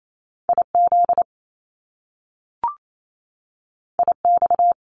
Morse code, heard as a keyed tone, about 28 wpm, 700 Hz. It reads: IZ IX
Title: Top Two Letter Combinations in Words 28wpm